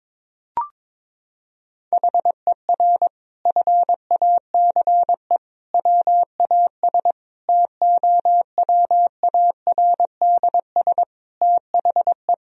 Morse code, heard as a keyed tone, about 22 wpm, 700 Hz. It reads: HER FACE WAS TOWARDS THE
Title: Sentences from Top 1000 Words - Review Part 2 - 22wpm